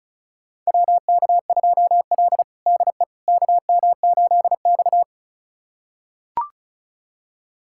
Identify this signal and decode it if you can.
Morse code, heard as a keyed tone, about 34 wpm, 700 Hz. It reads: WK2L DE KM8X
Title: USA Callsign DE Callsign - 4 Characters - 35wpm